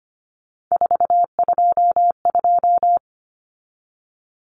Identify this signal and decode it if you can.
Morse code, heard as a keyed tone, about 25 wpm, 700 Hz. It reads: 422
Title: Common Cut Numbers - 3-Digit 25wpm